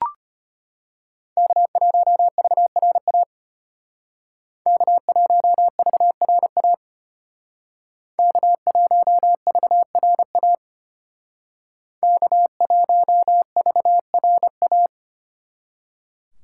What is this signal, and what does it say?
Morse code, heard as a keyed tone, about 30 wpm, 700 Hz. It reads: K1VRA K1VRA K1VRA K1VRA
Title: Speed Racing - USA 1x3 Call Signs 25wpm